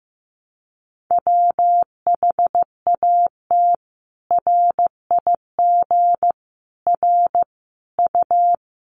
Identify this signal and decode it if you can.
Morse code, heard as a keyed tone, about 15 wpm, 700 Hz. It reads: WHAT RIG R U
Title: Common QSO Phrases 15wpm